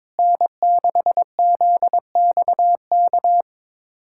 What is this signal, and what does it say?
Morse code, heard as a keyed tone, about 21 wpm, 700 Hz. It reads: N6ZXK
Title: USA Callsign DE Callsign - 5 Characters - 22wpm